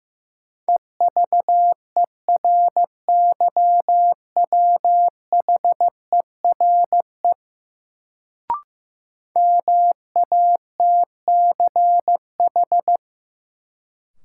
Morse code, heard as a keyed tone, about 15 wpm, 700 Hz. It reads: EVERYWHERE MATCH
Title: Sentences from Top 1600 Words - New Word Drill Alt - 15wpm